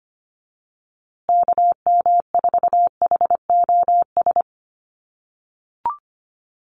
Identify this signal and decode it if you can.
Morse code, heard as a keyed tone, about 25 wpm, 700 Hz. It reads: KM4HOS
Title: USA Call Signs Repeated 3x 25wpm